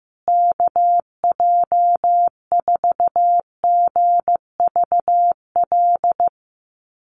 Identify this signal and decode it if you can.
Morse code, heard as a keyed tone, about 15 wpm, 700 Hz. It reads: KJ4GVL
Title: USA Call Signs Repeated 3x 15wpm